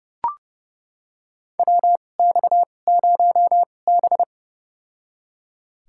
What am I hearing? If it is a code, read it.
Morse code, heard as a keyed tone, about 30 wpm, spaced as for 22 wpm, 700 Hz. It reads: WX0B
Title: USA Call Signs - 4 Characters - 2x Character Spacing - 30wpm